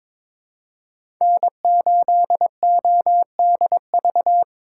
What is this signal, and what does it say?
Morse code, heard as a keyed tone, about 21 wpm, 700 Hz. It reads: N8ODV
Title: USA Call Signs - 8x Character Spacing - 22wpm